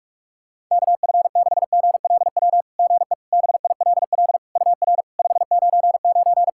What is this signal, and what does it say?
Morse code, heard as a keyed tone, about 48 wpm, 700 Hz. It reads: KF6GLW GE BILL UR 599
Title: SOTA QSO Practice 45wpm